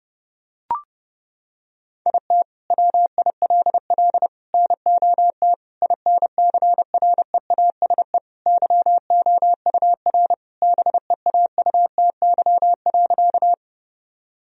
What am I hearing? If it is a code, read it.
Morse code, heard as a keyed tone, about 32 wpm, 700 Hz. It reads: IT WILL NOT INCREASE YOUR BEAUTY.
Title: Sentences from Top 1000 Words - Review Part 4 - 30wpm